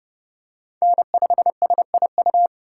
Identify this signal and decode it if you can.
Morse code, heard as a keyed tone, about 30 wpm, 700 Hz. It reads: N5SIU
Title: USA Call Signs - 3x Character Spacing - 30wpm